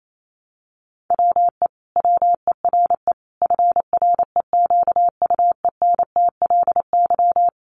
Morse code, heard as a keyed tone, about 28 wpm, 700 Hz. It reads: WE WERE FREQUENTLY
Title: Sentences from Top 1300 Words 28wpm